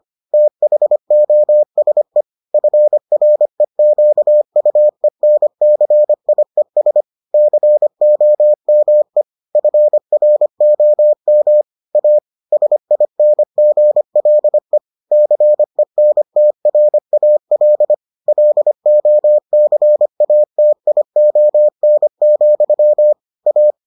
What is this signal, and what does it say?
Morse code, heard as a keyed tone, about 25 wpm, 600 Hz. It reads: THOSE FREQUENCIES COME FROM A SINGLE CENTRAL LOCATION, A